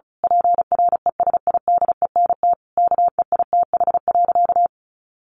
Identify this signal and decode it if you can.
Morse code, heard as a keyed tone, about 34 wpm, 700 Hz. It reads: PRESIDENT KEITH.
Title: A Sound Of Thunder 35wpm